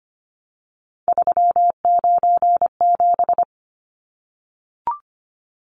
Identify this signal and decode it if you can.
Morse code, heard as a keyed tone, about 25 wpm, 700 Hz. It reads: 397
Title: Numbers - 3-Digit 25wpm